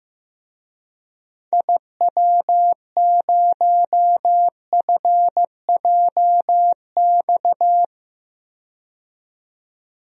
Morse code, heard as a keyed tone, about 15 wpm, 700 Hz. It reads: IW0FJX
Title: DXCC-100 Call Signs - 6 Characters - 15wpm